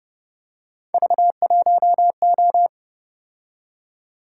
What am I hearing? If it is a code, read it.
Morse code, heard as a keyed tone, about 29 wpm, 700 Hz. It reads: V1O